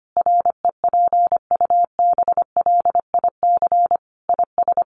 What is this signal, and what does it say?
Morse code, heard as a keyed tone, about 25 wpm, 700 Hz. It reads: REPUBLIC IS
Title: Sentences from Top 1500 Words 25wpm